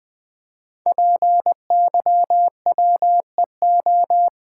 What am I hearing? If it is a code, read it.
Morse code, heard as a keyed tone, about 20 wpm, 700 Hz. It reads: PYWEO
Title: Code Groups - 5 Letters - 20wpm